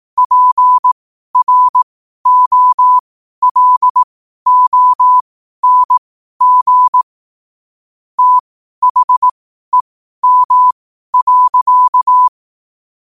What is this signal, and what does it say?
Morse code, heard as a keyed tone, about 18 wpm, spaced as for 12 wpm, 1000 Hz. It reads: PROLONG THEM.
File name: qotdfiles / 12wpm